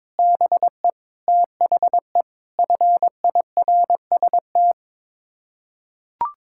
Morse code, heard as a keyed tone, about 22 wpm, 700 Hz. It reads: BE THE FIRST
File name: practice / 22wpm / Sets of 6 Words - 22wpm